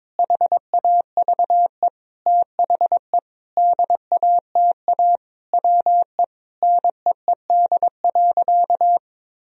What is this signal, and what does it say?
Morse code, heard as a keyed tone, about 22 wpm, 700 Hz. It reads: HAVE THE DATA WE NEED.
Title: Sentences from Top 1000 Words - Review Part 3 - 22wpm